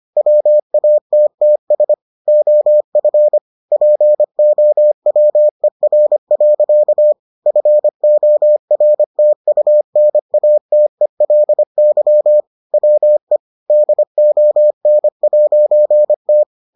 Morse code, heard as a keyed tone, about 25 wpm, 600 Hz. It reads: WATTS OF POWER. FORTUNATELY WE DON'T